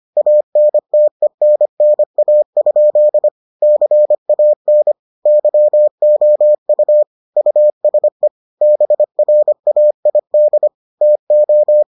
Morse code, heard as a keyed tone, about 25 wpm, 600 Hz. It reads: ANTENNA? CAN YOU USE BRAID TO